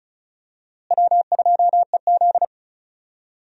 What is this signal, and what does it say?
Morse code, heard as a keyed tone, about 36 wpm, 700 Hz. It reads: W2EZ
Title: USA Call Signs - 4 Characters - Repeated 3x - 35wpm